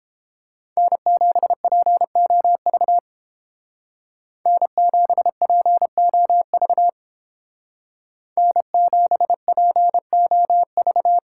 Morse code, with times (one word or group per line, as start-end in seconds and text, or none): 0.8-3.0 N7POV
4.5-6.9 N7POV
8.4-11.2 N7POV